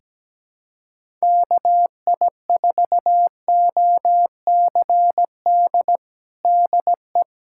Morse code, heard as a keyed tone, about 17 wpm, 700 Hz. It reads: KI4OCD DE
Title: USA Callsign DE Callsign - 6 Characters - 17wpm